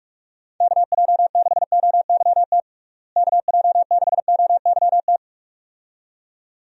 Morse code, heard as a keyed tone, about 46 wpm, 700 Hz. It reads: KJ6OYT KJ6OYT
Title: USA Call Signs - 6 Characters - Repeated 2x - 45wpm